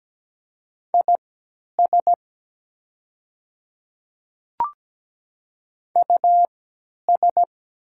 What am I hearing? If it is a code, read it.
Morse code, heard as a keyed tone, about 17 wpm, spaced as for 10 wpm, 700 Hz. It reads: IS US
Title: 3x Character Spacing - Top 200 Words - 17wpm